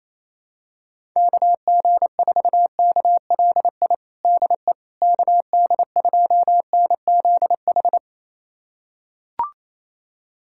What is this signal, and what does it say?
Morse code, heard as a keyed tone, about 28 wpm, 700 Hz. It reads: KG4KLI DE KD2NZH
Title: USA Callsign DE Callsign 28wpm